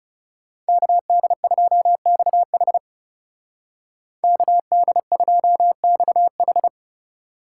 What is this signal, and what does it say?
Morse code, heard as a keyed tone, about 36 wpm, 700 Hz. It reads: KD2XH KD2XH
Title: Speed Racing - USA Call Signs 30wpm